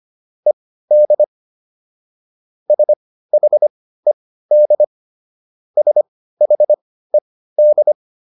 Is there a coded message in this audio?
Yes: Morse code from 0.5 to 7.9 s, heard as a keyed tone, about 25 wpm, spaced as for 15 wpm, 600 Hz. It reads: ED SHED SHED